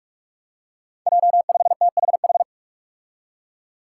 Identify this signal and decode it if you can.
Morse code, heard as a keyed tone, about 46 wpm, 700 Hz. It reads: J5THH
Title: Five Challenging Letter-Number Combinations 45wpm